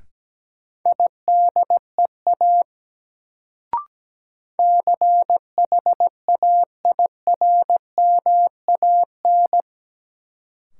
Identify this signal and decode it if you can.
Morse code, heard as a keyed tone, about 17 wpm, 700 Hz. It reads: IDEA CHAIRMAN
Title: Sets of 3 Words - New Word Drill - 17wpm